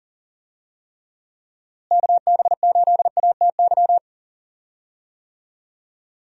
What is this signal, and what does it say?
Morse code, heard as a keyed tone, about 37 wpm, 700 Hz. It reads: KB8ATY
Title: USA Call Signs - 6 Characters - Repeated 2x - 40wpm